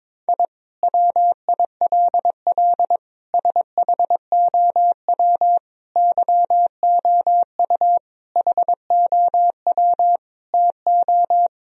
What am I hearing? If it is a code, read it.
Morse code, heard as a keyed tone, about 22 wpm, 700 Hz. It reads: I WILL SHOW YOU HOW TO
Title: Sentences from Top 1000 Words - Review Part 1 - 22wpm